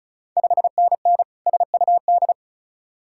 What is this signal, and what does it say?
Morse code, heard as a keyed tone, about 36 wpm, 700 Hz. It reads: 5NN SUD
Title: Ontario QSO Party Exchange 35wpm